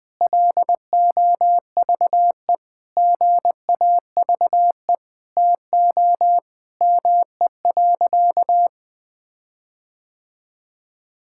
Morse code, heard as a keyed tone, about 20 wpm, 700 Hz. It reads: LOVE GAVE TO ME.
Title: Sentences with 8-Word Phrases - 20wpm